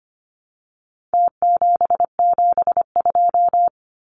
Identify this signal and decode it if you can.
Morse code, heard as a keyed tone, about 25 wpm, 700 Hz. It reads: T772
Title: Numbers - Contest Style - 4-Digit 25wpm